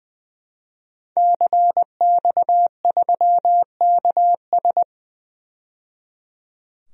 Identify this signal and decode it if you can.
Morse code, heard as a keyed tone, about 21 wpm, 700 Hz. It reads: CX3KS
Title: DXCC-100 Call Signs - 5 Characters - 20wpm